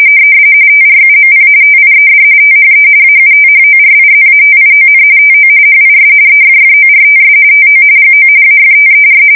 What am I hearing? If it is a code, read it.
RTTY, about 45.45 baud. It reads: TRUMP TIED THE AUTHORIZATION TO HIS EFFORTS TO GO AFTER
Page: 170 Shift 2125/2295